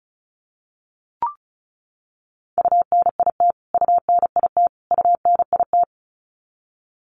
Morse code, heard as a keyed tone, about 34 wpm, 700 Hz. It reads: UNIT UNIT UNIT
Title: Top 500 Words Repeated 35wpm